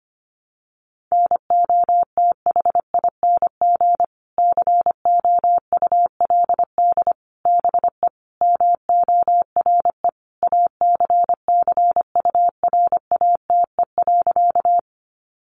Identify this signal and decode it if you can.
Morse code, heard as a keyed tone, about 25 wpm, 700 Hz. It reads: NOTHING COULD BE MORE ACCURATE.
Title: Sentences from Top 1200 Words 25wpm